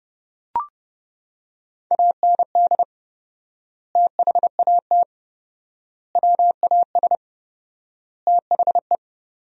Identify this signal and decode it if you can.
Morse code, heard as a keyed tone, about 30 wpm, 700 Hz. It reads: AND THAT WAS THE
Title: 4x Word-Spacing - Sentences from Top 200 Words - 30wpm